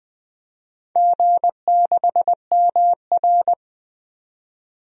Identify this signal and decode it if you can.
Morse code, heard as a keyed tone, about 20 wpm, 700 Hz. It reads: G6MR
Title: DXCC-100 Call Signs - 4 Characters - 20wpm